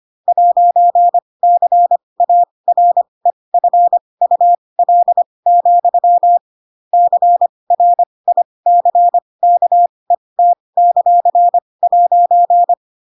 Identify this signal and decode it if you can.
Morse code, heard as a keyed tone, about 25 wpm, spaced as for 20 wpm, 700 Hz. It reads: 'CAREFUL, CRICKET;'
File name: Pinocchio / Pinocchio ch7 15 25 ik1hge